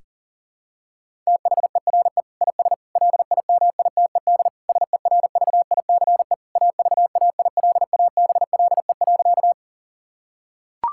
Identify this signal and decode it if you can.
Morse code, heard as a keyed tone, about 40 wpm, 690 Hz. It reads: THERE IS LIMITED SERVICE AVAILABLE.
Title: Sentences from Top 1000 Words - Review Part 6 - 40wpm